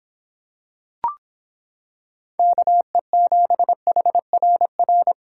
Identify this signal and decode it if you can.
Morse code, heard as a keyed tone, about 26 wpm, 700 Hz. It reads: KE7HRR